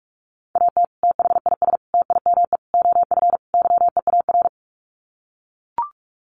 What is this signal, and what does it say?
Morse code, heard as a keyed tone, about 44 wpm, 700 Hz. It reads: AT THIS TIME OF YEAR
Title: Sets of 5 Words Encore2 45wpm